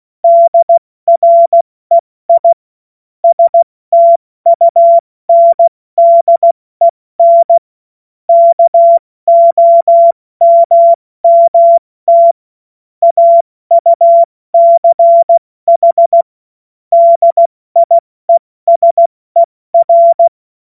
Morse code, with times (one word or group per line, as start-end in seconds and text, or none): 0.2-2.5 DREI
3.2-7.6 STUNDEN
8.3-12.3 KOMMT
13.0-16.2 AUCH
16.9-20.3 DIESER